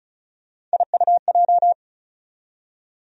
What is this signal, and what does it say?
Morse code, heard as a keyed tone, about 34 wpm, 700 Hz. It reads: IUJ
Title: USA Call Sign Postfixes - 3 Letters 35wpm